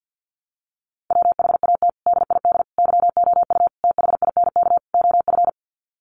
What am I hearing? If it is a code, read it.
Morse code, heard as a keyed tone, about 49 wpm, 700 Hz. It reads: WHAT DID YOU THINK OF
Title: Sets of 5 Words Encore2 50wpm